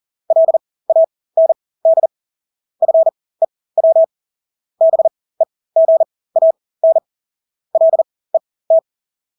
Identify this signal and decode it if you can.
Morse code, heard as a keyed tone, about 40 wpm, spaced as for 20 wpm, 650 Hz. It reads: LAND FEW BEGAN LET
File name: Engelske ord 40-20 wpm